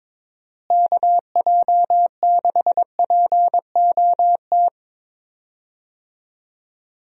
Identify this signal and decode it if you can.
Morse code, heard as a keyed tone, about 22 wpm, 700 Hz. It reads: KJ6POT
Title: USA Call Signs - 6 Characters - Repeated 2x - 22wpm